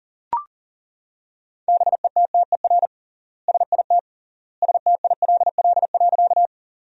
Morse code, heard as a keyed tone, about 37 wpm, 700 Hz. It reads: BETTER SIT STILL.
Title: 3x Word-Spacing - Sentences from Top 500 Words - 40wpm